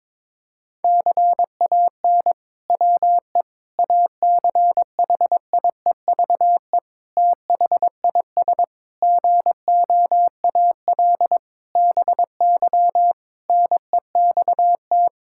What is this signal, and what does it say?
Morse code, heard as a keyed tone, about 22 wpm, 700 Hz. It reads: CAN WE ACHIEVE THIS GOAL BY NEXT
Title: Sentences from Top 1000 Words - Review Part 5 - 22wpm